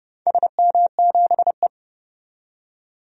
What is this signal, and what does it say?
Morse code, heard as a keyed tone, about 30 wpm, 700 Hz. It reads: SM7E
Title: DXCC-100 Call Signs - 4 Characters - 30wpm